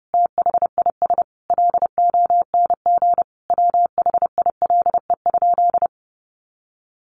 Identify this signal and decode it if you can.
Morse code, heard as a keyed tone, about 30 wpm, 700 Hz. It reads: THIS LONG WHILE?
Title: Sentences from Top 300 Words 30wpm